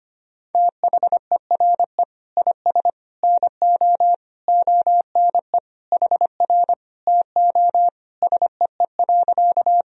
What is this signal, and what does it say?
Morse code, heard as a keyed tone, about 25 wpm, 700 Hz. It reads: THERE IS NO ONE HR TO SEE.
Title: Sentences from Top 200 Words and Abbreviations 25wpm